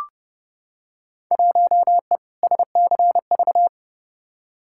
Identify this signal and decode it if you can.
Morse code, heard as a keyed tone, about 32 wpm, 700 Hz. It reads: 1E SCV
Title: Field Day Exchanges - 30wpm